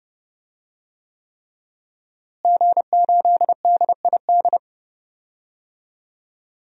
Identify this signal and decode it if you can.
Morse code, heard as a keyed tone, about 30 wpm, 700 Hz. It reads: G8DID
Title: DXCC-100 Call Signs - 5 Characters - Repeated 3x - 30wpm